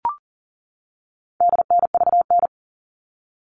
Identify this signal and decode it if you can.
Morse code, heard as a keyed tone, about 40 wpm, 700 Hz. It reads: DNVN
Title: All Cut Numbers - 4-Digit 40wpm